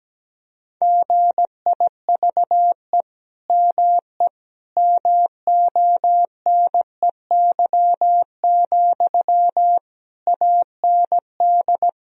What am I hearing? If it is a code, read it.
Morse code, heard as a keyed tone, about 17 wpm, 700 Hz. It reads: GIVE ME MONEY, AND
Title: Sentences from Top 1000 Words - Review Part 2 - 17wpm